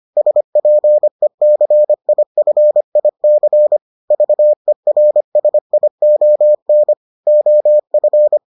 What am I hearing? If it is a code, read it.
Morse code, heard as a keyed tone, about 25 wpm, 600 Hz. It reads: SPECIFIC VERSION OF